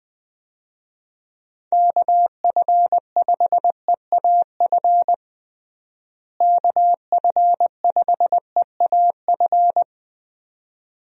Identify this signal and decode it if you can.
Morse code, heard as a keyed tone, about 20 wpm, 700 Hz. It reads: KF5EAF KF5EAF
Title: USA Call Signs - 6 Characters - Repeated 2x - 20wpm